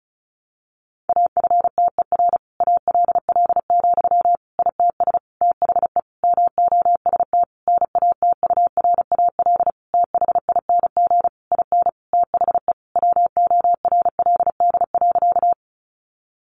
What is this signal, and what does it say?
Morse code, heard as a keyed tone, about 34 wpm, 700 Hz. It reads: AFTER ALL, ITS THE MOST NATURAL THING IN THE WORLD.
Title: Sentences from Top 700 Words 35wpm